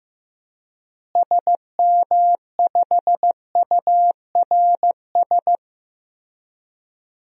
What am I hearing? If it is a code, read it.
Morse code, heard as a keyed tone, about 15 wpm, 700 Hz. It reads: SM5URS
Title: DXCC-100 Call Signs - 6 Characters - Repeated 3x - 15wpm